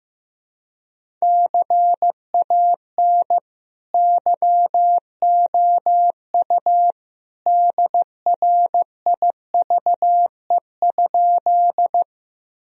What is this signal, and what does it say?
Morse code, heard as a keyed tone, about 15 wpm, 700 Hz. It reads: CAN YOU DRIVE?
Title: Sentences from Top 1000 Words - Review Part 2 - 15wpm